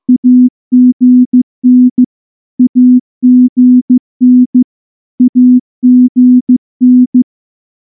Morse code, heard as a keyed tone, about 14 wpm, 250 Hz. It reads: AGN AGN AGN
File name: morse / morse-training-15WPM